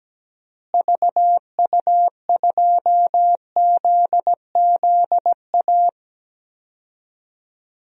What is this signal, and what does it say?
Morse code, heard as a keyed tone, about 17 wpm, 700 Hz. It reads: VU2ZZA
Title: DXCC-100 Call Signs - 6 Characters - Repeated 3x - 17wpm